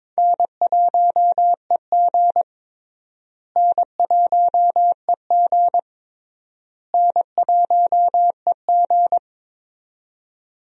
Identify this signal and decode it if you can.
Morse code, heard as a keyed tone, about 22 wpm, 700 Hz. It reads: N1EG N1EG N1EG
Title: USA Call Signs - 4 Characters - Repeated 3x - 22wpm